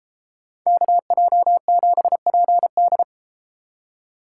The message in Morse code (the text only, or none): KJ7PD